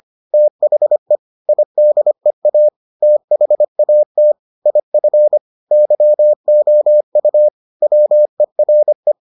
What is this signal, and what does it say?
Morse code, heard as a keyed tone, about 25 wpm, 600 Hz. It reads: THE IDEA THAT IF YOU WERE